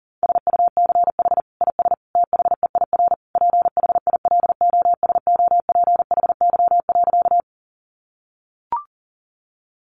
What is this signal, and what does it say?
Morse code, heard as a keyed tone, about 40 wpm, 700 Hz. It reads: SUCH IS THEIR PHILOSOPHY.
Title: Sentences from Top 900 Words 40wpm